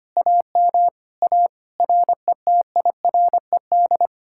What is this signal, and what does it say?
Morse code, heard as a keyed tone, about 33 wpm, 700 Hz. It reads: AM A RETIRED